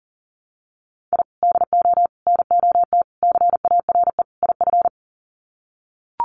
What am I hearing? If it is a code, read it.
Morse code, heard as a keyed tone, about 40 wpm, 700 Hz. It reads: I DO NOT CARE IF
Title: Sets of 5 Words 40wpm